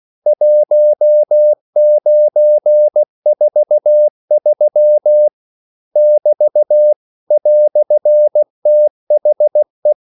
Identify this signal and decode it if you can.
Morse code, heard as a keyed tone, about 16 wpm, 600 Hz. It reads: 1943 <BT> "THE
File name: CW / Practice / EnglishQuotations0416